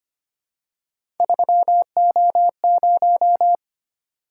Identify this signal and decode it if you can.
Morse code, heard as a keyed tone, about 25 wpm, 700 Hz. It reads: 3O0
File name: Three Letter-Number Combinations - 25wpm